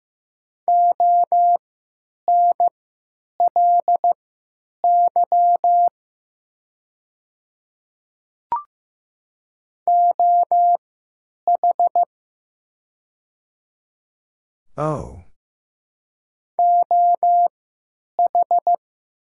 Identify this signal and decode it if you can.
Morse code, heard as a keyed tone, about 15 wpm, spaced as for 9 wpm, 700 Hz. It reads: ONLY OH OH
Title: 3x Character Spacing - Top 500 Words - 15wpm